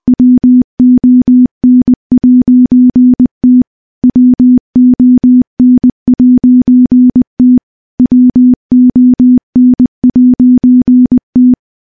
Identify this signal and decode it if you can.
Morse code, heard as a keyed tone, about 21 wpm, 250 Hz. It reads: WON'T WON'T WON'T
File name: morse / morse-training-20WPM